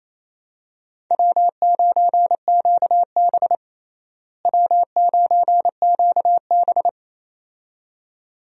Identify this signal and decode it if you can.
Morse code, heard as a keyed tone, about 28 wpm, 700 Hz. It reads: W9QB W9QB
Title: USA Call Signs - 4 Characters - Repeated 2x - 28wpm